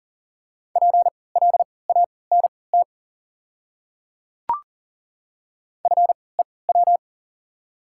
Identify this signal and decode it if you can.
Morse code, heard as a keyed tone, about 40 wpm, spaced as for 23 wpm, 690 Hz. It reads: PLANT FEW
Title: 3x Character Spacing - Top 300 Words - 40wpm